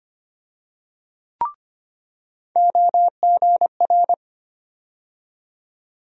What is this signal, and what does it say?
Morse code, heard as a keyed tone, about 25 wpm, 700 Hz. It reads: OGR
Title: USA Call Sign Postfixes - 3 Letters 25wpm